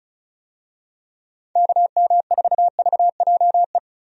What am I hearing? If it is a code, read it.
Morse code, heard as a keyed tone, about 36 wpm, 700 Hz. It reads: KM4VJE
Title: USA Call Signs - 6 Characters - 35wpm